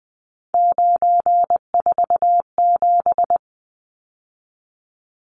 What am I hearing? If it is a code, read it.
Morse code, heard as a keyed tone, about 20 wpm, 700 Hz. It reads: 947
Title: Numbers - 3-Digit 20wpm